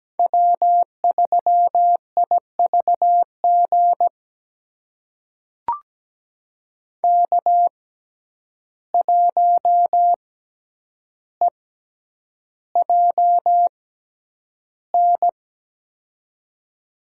Morse code, heard as a keyed tone, about 17 wpm, spaced as for 9 wpm, 700 Hz. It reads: W3IVG K1EJN
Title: USA Call Signs - 5 Characters - 6x Character Spacing - 17wpm